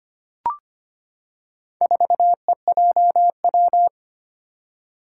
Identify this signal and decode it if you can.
Morse code, heard as a keyed tone, about 25 wpm, 700 Hz. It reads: 4EJW
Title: Four Challenging Letter-Number Combinations 25wpm